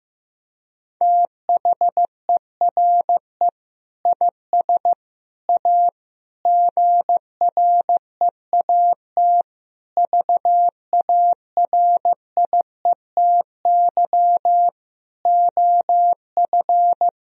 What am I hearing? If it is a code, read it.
Morse code, heard as a keyed tone, about 15 wpm, 700 Hz. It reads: THERE IS A GREAT VARIETY OF
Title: Sentences from Top 1000 Words - Review Part 4 - 15wpm